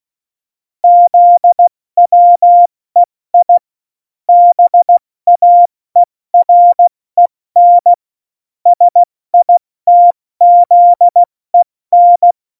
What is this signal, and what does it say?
Morse code, heard as a keyed tone, about 16 wpm, 700 Hz. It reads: ZWEI BAEREN SITZEN
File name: Morsekurs / Witze / Witz 29. Mai 2018, 70 F80